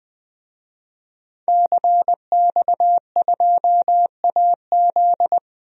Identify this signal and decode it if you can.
Morse code, heard as a keyed tone, about 20 wpm, 700 Hz. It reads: CX2AZ
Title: DXCC-100 Call Signs - 5 Characters - 20wpm